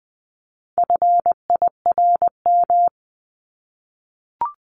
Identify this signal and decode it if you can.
Morse code, heard as a keyed tone, about 20 wpm, 700 Hz. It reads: FIRM
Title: Top 1251-1500 Words 20wpm